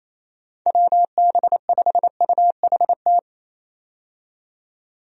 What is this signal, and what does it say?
Morse code, heard as a keyed tone, about 27 wpm, 700 Hz. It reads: WB5UHT
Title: USA Call Signs - 6 Characters - 28wpm